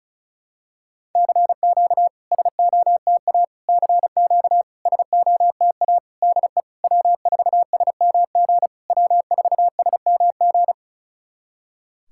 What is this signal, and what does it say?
Morse code, heard as a keyed tone, about 36 wpm, 700 Hz. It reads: CQ SOTA CQ SOTA DE W4SMG W4SMG
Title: SOTA QSO Practice 35wpm